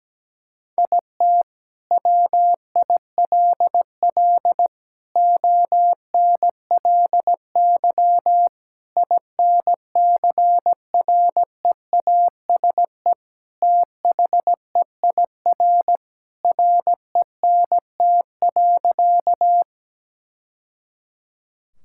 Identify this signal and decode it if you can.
Morse code, heard as a keyed tone, about 17 wpm, 700 Hz. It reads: IT WILL ONLY INCREASE THEIR RENT.
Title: Sentences from Top 1000 Words - Review Part 5 - 17wpm